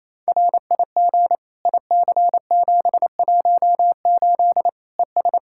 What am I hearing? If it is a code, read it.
Morse code, heard as a keyed tone, about 28 wpm, 700 Hz. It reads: RIG IC718 ES
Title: Practice QSO - Exchange 2 - Info - 28wpm